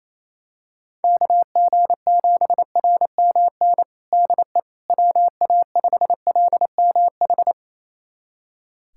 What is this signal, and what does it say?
Morse code, heard as a keyed tone, about 28 wpm, 700 Hz. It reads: KG7RMN DE WA5LMH
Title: USA Callsign DE Callsign - 6 Characters - 28wpm